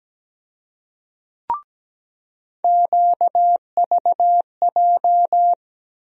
Morse code, heard as a keyed tone, about 17 wpm, 700 Hz. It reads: QVJ
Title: USA Call Sign Postfixes - 3 Letters 17wpm